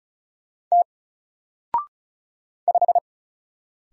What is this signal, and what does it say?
Morse code, heard as a keyed tone, about 35 wpm, 700 Hz. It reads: T 5
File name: Single-Letter-Number-35wpm